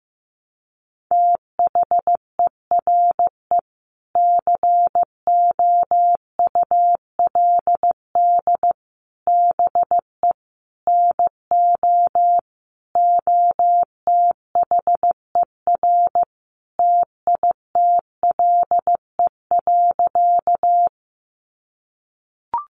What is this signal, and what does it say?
Morse code, heard as a keyed tone, about 14 wpm, 700 Hz. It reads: THERE COULD BE NO OTHER TITLE.
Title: Sentences from Top 1200 Words 15wpm